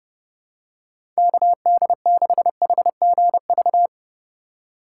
Morse code, heard as a keyed tone, about 29 wpm, 700 Hz. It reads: KD6HGV
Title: USA Call Signs - 6 Characters - 2x Character Spacing - 30wpm